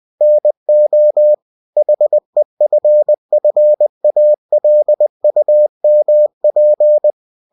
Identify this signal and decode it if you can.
Morse code, heard as a keyed tone, about 20 wpm, 600 Hz. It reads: NO HEFFALUMP
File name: cw-books / winnie-the-pooh / winnie-20wpm-page-0041